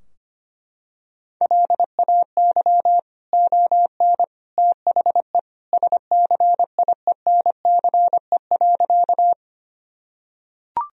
Morse code, heard as a keyed tone, about 25 wpm, 700 Hz. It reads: LAY ON THE SCIENCE.
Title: Sentences from Top 1000 Words - Review Part 2 - 25wpm